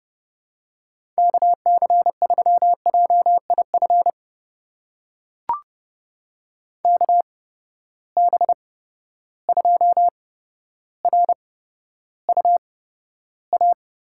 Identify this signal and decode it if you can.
Morse code, heard as a keyed tone, about 32 wpm, spaced as for 13 wpm, 700 Hz. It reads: KC3JIF KB2RUA
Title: USA Call Signs - 6 Characters - 8x Character Spacing - 30wpm